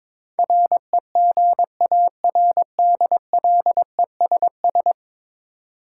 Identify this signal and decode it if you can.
Morse code, heard as a keyed tone, about 22 wpm, 700 Hz. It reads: REGARDLESS
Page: morse-code-ninja-practice-22wpm